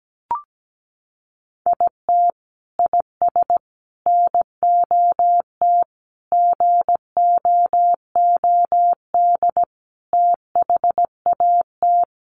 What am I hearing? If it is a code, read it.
Morse code, heard as a keyed tone, about 17 wpm, 700 Hz. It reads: IT IS NOT GOOD THAT
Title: Sentences from Top 600 Words 17wpm